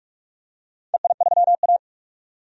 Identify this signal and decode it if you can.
Morse code, heard as a keyed tone, about 46 wpm, 700 Hz. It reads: EI3A
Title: DXCC-100 Call Signs - 4 Characters - 45wpm